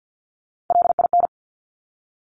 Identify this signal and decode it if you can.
Morse code, heard as a keyed tone, about 50 wpm, 700 Hz. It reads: LIN
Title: Top Three Letter Combinations in Words 50wpm